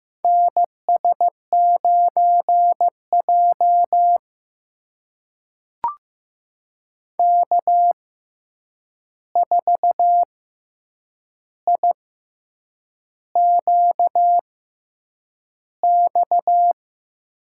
Morse code, heard as a keyed tone, about 15 wpm, spaced as for 7 wpm, 700 Hz. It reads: NS9J K4IQX
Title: USA Call Signs - 6x Character Spacing - 15wpm